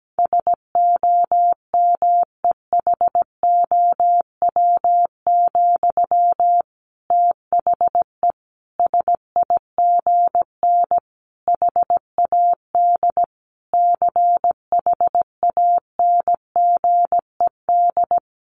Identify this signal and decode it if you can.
Morse code, heard as a keyed tone, about 17 wpm, 700 Hz. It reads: SOMEHOW, THE SIGN HAD CHANGED
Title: A Sound Of Thunder 17wpm